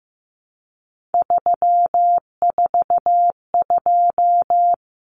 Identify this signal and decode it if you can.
Morse code, heard as a keyed tone, about 15 wpm, 700 Hz. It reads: 342
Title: Numbers - 3-Digit 15wpm